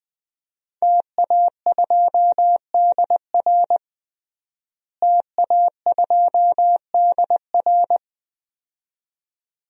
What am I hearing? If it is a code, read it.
Morse code, heard as a keyed tone, about 20 wpm, 700 Hz. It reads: TA2DR TA2DR
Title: DXCC-100 Call Signs - 5 Characters - Repeated 2x - 20wpm